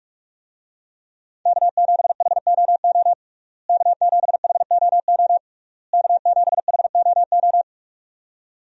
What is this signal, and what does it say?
Morse code, heard as a keyed tone, about 44 wpm, 700 Hz. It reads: K7HOO K7HOO K7HOO
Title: USA Call Signs - 5 Characters - Repeated 3x - 45wpm